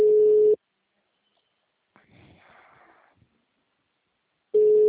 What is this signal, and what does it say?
Ringback tone (Europe, 425 Hz).